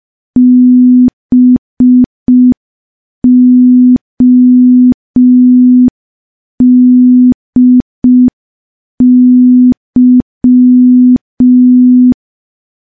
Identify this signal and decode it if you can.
Morse code, heard as a keyed tone, about 5 wpm, 250 Hz. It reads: BODY
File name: morse / morse-training-05WPM